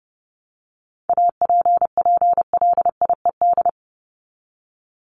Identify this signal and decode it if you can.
Morse code, heard as a keyed tone, about 30 wpm, 700 Hz. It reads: APPLIED
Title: ED Suffix - Top 100 Words with Base Words 30wpm